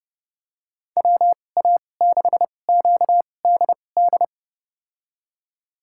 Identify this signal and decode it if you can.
Morse code, heard as a keyed tone, about 30 wpm, spaced as for 22 wpm, 700 Hz. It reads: WA6QDD
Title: USA Call Signs - 6 Characters - 2x Character Spacing - 30wpm